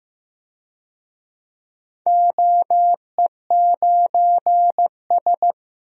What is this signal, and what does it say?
Morse code, heard as a keyed tone, about 15 wpm, 700 Hz. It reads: OE9S
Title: DXCC-100 Call Signs - 4 Characters - Repeated 3x - 15wpm